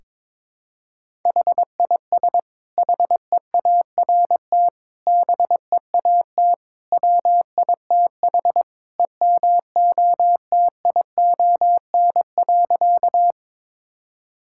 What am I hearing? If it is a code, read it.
Morse code, heard as a keyed tone, about 22 wpm, 700 Hz. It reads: HIS HEART BEAT WITH EMOTION.
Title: Sentences from Top 1600 Words 22wpm